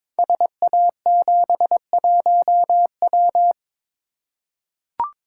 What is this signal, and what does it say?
Morse code, heard as a keyed tone, about 22 wpm, 700 Hz. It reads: SA71W
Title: Code Groups - 5 Letter:Numbers - 22wpm